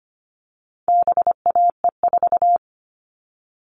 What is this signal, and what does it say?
Morse code, heard as a keyed tone, about 25 wpm, 700 Hz. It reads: BAE4
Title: All Cut Numbers - 4-Digit 25wpm